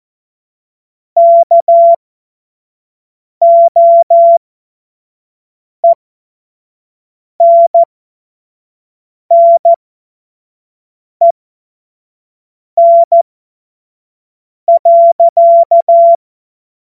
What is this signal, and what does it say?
Morse code, heard as a keyed tone, about 14 wpm, spaced as for 5 wpm, 680 Hz. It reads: KOENNEN.
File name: Morsekurs / Witze / Witz 11. Mai 2021, 25 F70